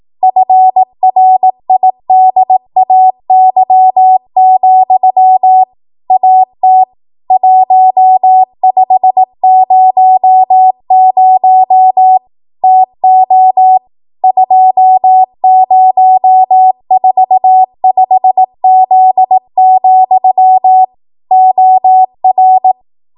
Morse code, heard as a keyed tone, about 18 wpm, 750 Hz. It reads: FRIDAY, AT 1500 TO 2045Z, OR
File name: Archive / 18 WPM / 18WPM